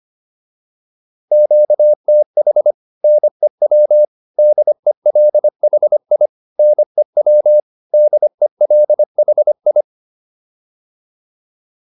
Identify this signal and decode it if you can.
Morse code, heard as a keyed tone, about 25 wpm, 600 Hz. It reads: QTH NEW DELHI NEW DELHI